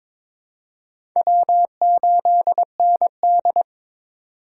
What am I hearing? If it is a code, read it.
Morse code, heard as a keyed tone, about 22 wpm, 700 Hz. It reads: W8ND
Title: USA Call Signs - 8x Character Spacing - 22wpm